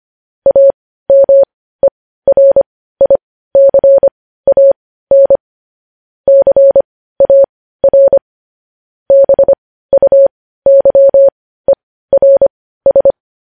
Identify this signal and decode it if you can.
Morse code, heard as a keyed tone, about 25 wpm, spaced as for 15 wpm, 550 Hz. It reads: AMERICAN CAR BUYERS
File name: podcast / News headlines-1x-25x15